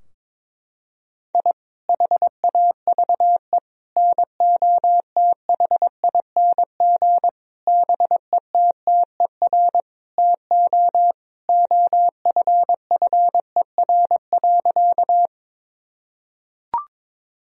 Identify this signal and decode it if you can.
Morse code, heard as a keyed tone, about 23 wpm, 700 Hz. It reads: I HAVE NOTHING BETTER TO OFFER.
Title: Sentences from Top 1000 Words - Review Part 4 - 22wpm